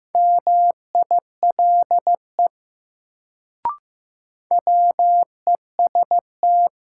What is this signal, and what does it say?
Morse code, heard as a keyed tone, about 15 wpm, 700 Hz. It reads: MILE WEST
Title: Sentences from Top 400 Words - New Word Drill - 15wpm